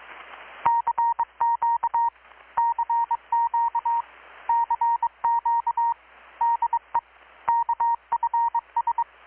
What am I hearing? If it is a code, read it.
Morse code, heard as a keyed tone, about 21 wpm, 930 Hz. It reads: CQ CQ CQ DE KFS
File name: blog / swl / 6-abril-2019-12695-USB-KPS